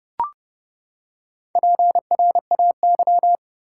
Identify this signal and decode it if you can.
Morse code, heard as a keyed tone, about 31 wpm, 700 Hz. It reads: PRAY
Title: Four Letter Words - 30wpm